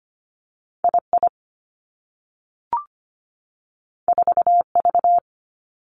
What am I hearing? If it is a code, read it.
Morse code, heard as a keyed tone, about 25 wpm, 700 Hz. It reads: II 4V
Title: Two Challenging Letter-Number Combinations 25wpm